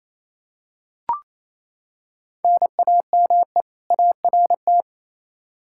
Morse code, heard as a keyed tone, about 28 wpm, 700 Hz. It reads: NAME ART
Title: K1USN SST QSO Practice 28wpm